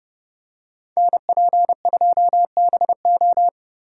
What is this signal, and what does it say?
Morse code, heard as a keyed tone, about 29 wpm, 700 Hz. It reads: NP2BO